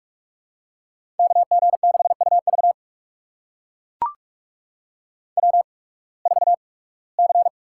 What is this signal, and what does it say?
Morse code, heard as a keyed tone, about 44 wpm, spaced as for 23 wpm, 690 Hz. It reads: KG6UV W4C
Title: USA Call Signs - 5 Characters - 8x Character Spacing - 45wpm